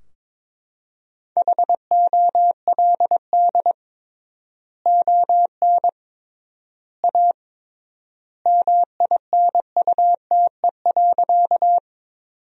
Morse code, heard as a keyed tone, about 22 wpm, 700 Hz. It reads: HOLD ON A MINUTE.
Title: 3x Word-Spacing - Sentences from Top 500 Words - 22wpm